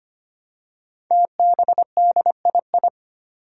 Morse code, heard as a keyed tone, about 25 wpm, 700 Hz. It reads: TBDII
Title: Five Challenging Letter-Number Combinations 25wpm